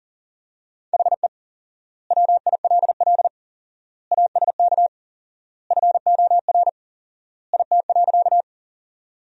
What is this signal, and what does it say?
Morse code, heard as a keyed tone, about 40 wpm, 700 Hz. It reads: HE WILL ASK FOR IT.
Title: 4x Word-Spacing - Sentences from Top 200 Words - 40wpm